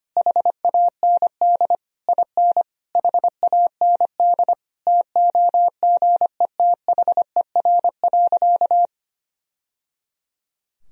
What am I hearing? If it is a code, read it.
Morse code, heard as a keyed tone, about 25 wpm, 700 Hz. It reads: HAND IN HAND TOGETHER.
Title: Sentences from Top 1000 Words - Review Part 4 - 25wpm